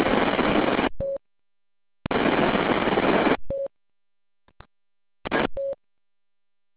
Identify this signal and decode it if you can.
There is no code.